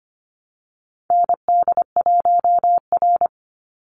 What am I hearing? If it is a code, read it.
Morse code, heard as a keyed tone, about 25 wpm, 700 Hz. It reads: ND1R
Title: Call Signs - 25wpm